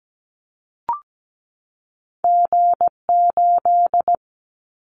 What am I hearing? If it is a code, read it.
Morse code, heard as a keyed tone, about 17 wpm, 700 Hz. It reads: G8
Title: Callsign Prefixes for Top 25 DXCC - 17wpm